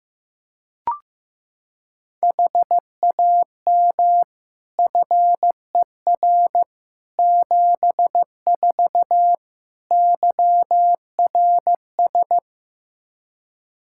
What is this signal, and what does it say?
Morse code, heard as a keyed tone, about 16 wpm, 700 Hz. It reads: HAM FER 74 YRS
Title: Practice QSO - Exchange 3 - More Info - 15wpm